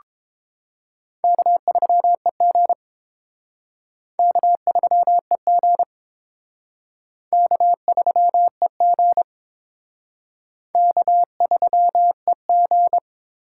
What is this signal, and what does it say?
Morse code, heard as a keyed tone, about 29 wpm, 700 Hz. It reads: K3EG K3EG K3EG K3EG